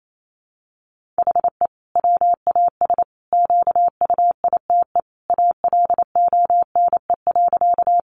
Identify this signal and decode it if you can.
Morse code, heard as a keyed tone, about 28 wpm, 700 Hz. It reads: HE WAS QUITE ALONE.
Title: Sentences from Top 1000 Words 28wpm